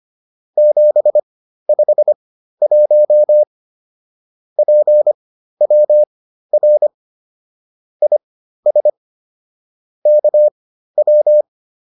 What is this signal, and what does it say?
Morse code, heard as a keyed tone, about 25 wpm, spaced as for 13 wpm, 600 Hz. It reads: 751 PWR IS KW